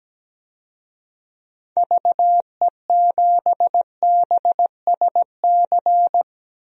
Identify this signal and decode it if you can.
Morse code, heard as a keyed tone, about 17 wpm, 700 Hz. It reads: VE7BSC
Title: DXCC-100 Call Signs - 6 Characters - Repeated 2x - 17wpm